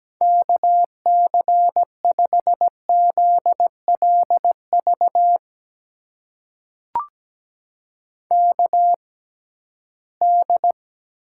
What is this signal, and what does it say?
Morse code, heard as a keyed tone, about 17 wpm, spaced as for 13 wpm, 700 Hz. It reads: KC5ZLV KD
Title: USA Call Signs - 6 Characters - 6x Character Spacing - 17wpm